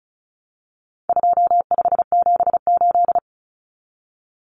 Morse code, heard as a keyed tone, about 34 wpm, 700 Hz. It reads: 2578
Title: Common Cut Numbers - 4-Digit 35wpm